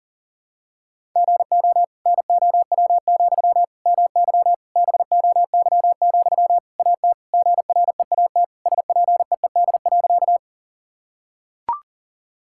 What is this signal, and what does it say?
Morse code, heard as a keyed tone, about 40 wpm, 700 Hz. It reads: GO NOW, MY BOY, AT GREAT SPEED.
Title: Sentences from Top 1000 Words - Review Part 5 - 40wpm